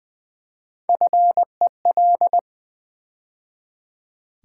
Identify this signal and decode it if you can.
Morse code, heard as a keyed tone, about 20 wpm, 700 Hz. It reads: FEL
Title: Build A Word - 3 Letters At A Time - 6-Letter Words - 20wpm